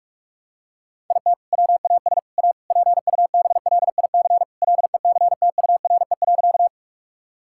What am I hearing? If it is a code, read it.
Morse code, heard as a keyed tone, about 47 wpm, 700 Hz. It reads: IT WAS A PUBLIC LECTURE.
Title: Sentences from Top 1600 Words 45wpm